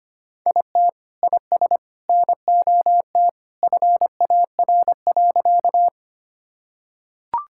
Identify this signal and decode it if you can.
Morse code, heard as a keyed tone, about 25 wpm, 700 Hz. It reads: IT IS NOT FAR.
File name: practice / 25wpm / Sentences with 2-Word Phrases - 25wpm